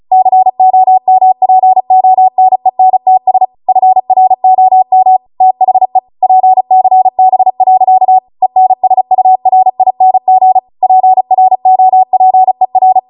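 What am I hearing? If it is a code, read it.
Morse code, heard as a keyed tone, about 36 wpm, 750 Hz. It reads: COMPONENTS FROM THE PCB. ENSURING PROPER
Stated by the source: Amateur radio Morse code practice archive transmission at 35 wpm.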